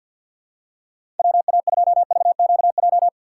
Morse code, heard as a keyed tone, about 49 wpm, 700 Hz. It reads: WA2VQJ
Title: USA Call Signs - 4x Character Spacing - 50wpm